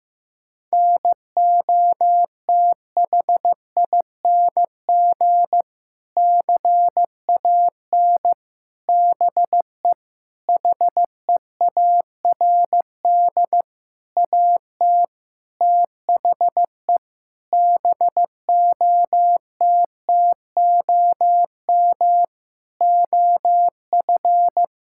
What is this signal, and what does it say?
Morse code, heard as a keyed tone, about 15 wpm, 700 Hz. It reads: NOTHING CAN BE HEARD AT THE BOTTOM OF
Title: Sentences from Top 1000 Words - Review Part 6 - 15wpm